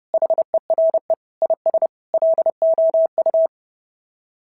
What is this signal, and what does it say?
Morse code, heard as a keyed tone, about 30 wpm, 690 Hz. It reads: HERE IS LOU
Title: SOTA QSO Practice 30wpm